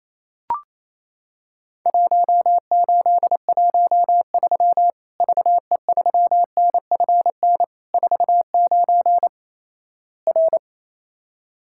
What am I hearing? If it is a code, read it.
Morse code, heard as a keyed tone, about 28 wpm, 700 Hz. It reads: 1813 VE3NFN 49 R
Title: Worked All Europe DX Contest - QTC Exchange - 28wpm